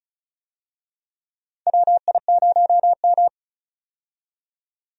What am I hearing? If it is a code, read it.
Morse code, heard as a keyed tone, about 36 wpm, 700 Hz. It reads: WI0M
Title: USA Call Signs - 4 Characters - 35wpm